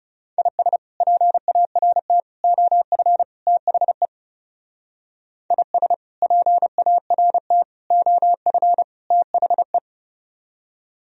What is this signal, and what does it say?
Morse code, heard as a keyed tone, about 40 wpm, 700 Hz. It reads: IS PART OF THE IS PART OF THE